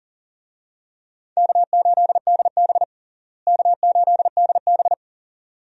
Morse code, heard as a keyed tone, about 40 wpm, 700 Hz. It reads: K8DB K8DB
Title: USA Call Signs - 4 Characters - Repeated 2x - 40wpm